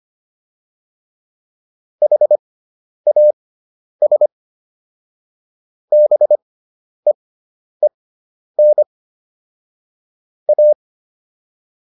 Morse code, heard as a keyed tone, about 25 wpm, spaced as for 10 wpm, 600 Hz. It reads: HAS BEEN A